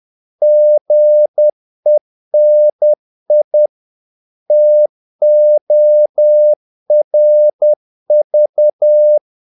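Morse code, heard as a keyed tone, about 10 wpm, 600 Hz. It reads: GENI TORV
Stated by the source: Morse practice of short Danish words at 10 wpm, from CW med Gnister.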